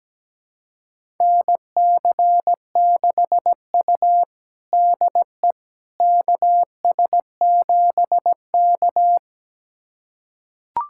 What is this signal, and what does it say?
Morse code, heard as a keyed tone, about 17 wpm, 700 Hz. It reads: NC6U DE KS7K
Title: USA Callsign DE Callsign - 4 Characters - 17wpm